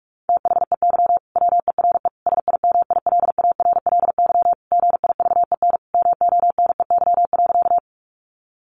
Morse code, heard as a keyed tone, about 44 wpm, 700 Hz. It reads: THEY WERE SIMILARLY GIVEN MONEY.
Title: Sentences from Top 1400 Words 45wpm